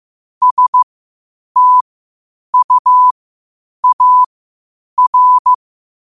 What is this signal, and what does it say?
Morse code, heard as a keyed tone, about 15 wpm, spaced as for 8 wpm, 1000 Hz. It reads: STUAR
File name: qotdfiles / 8wpm